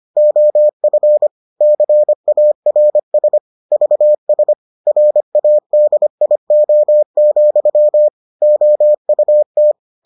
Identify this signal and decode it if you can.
Morse code, heard as a keyed tone, about 25 wpm, 600 Hz. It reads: OF CARS VS RADIO, OUT